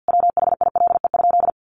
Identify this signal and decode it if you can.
Morse code, heard as a keyed tone, about 50 wpm, 700 Hz. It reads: WHILE?
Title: Sentences from Top 300 Words 50wpm